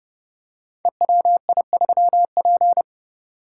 Morse code, heard as a keyed tone, about 28 wpm, 700 Hz. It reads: EWI3P
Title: Code Groups - 5 Letter:Numbers - 30wpm